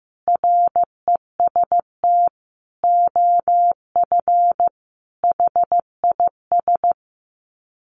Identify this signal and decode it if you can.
Morse code, heard as a keyed tone, about 15 wpm, 700 Hz. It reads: REST OF HIS
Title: Sets of 5 Words 15wpm